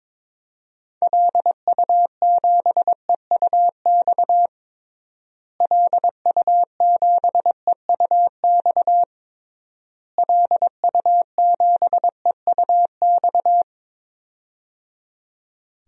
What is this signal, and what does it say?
Morse code, heard as a keyed tone, about 22 wpm, 700 Hz. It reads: LU7EUX LU7EUX LU7EUX
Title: DXCC-100 Call Signs - 6 Characters - Repeated 3x - 22wpm